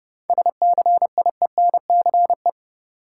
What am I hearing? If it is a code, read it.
Morse code, heard as a keyed tone, about 30 wpm, 700 Hz. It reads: SCIENCE
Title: Sentences from Top 1000 Words - Review Part 2 - New Word Drill - 30wpm